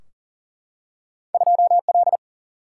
Morse code, heard as a keyed tone, about 40 wpm, 700 Hz. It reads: 2L
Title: Single-Letter Single-Number 40wpm